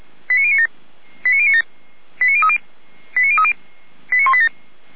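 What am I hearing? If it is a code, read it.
Five-tone selective call (ZVEI), heard as quick runs of five tones, tones of about 80 ms. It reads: 89097 89097 89030 89030 89187
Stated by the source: ZVEI five-tone call received by SDR.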